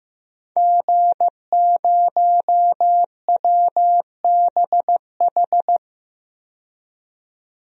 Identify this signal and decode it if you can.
Morse code, heard as a keyed tone, about 16 wpm, 700 Hz. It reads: G0WBH
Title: DXCC-100 Call Signs - 5 Characters - 15wpm